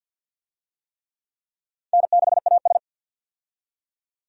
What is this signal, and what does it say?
Morse code, heard as a keyed tone, about 52 wpm, 700 Hz. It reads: N6AS
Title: USA Call Signs - 4 Characters - 50wpm